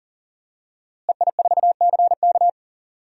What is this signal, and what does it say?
Morse code, heard as a keyed tone, about 40 wpm, 700 Hz. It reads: EI4CK
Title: DXCC-100 Call Signs - 5 Characters - Repeated 3x - 40wpm